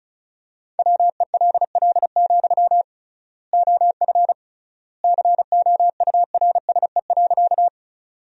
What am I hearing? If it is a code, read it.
Morse code, heard as a keyed tone, about 34 wpm, 700 Hz. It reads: WELL, OF COURSE.
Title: 3x Word-Spacing - Sentences from Top 500 Words - 35wpm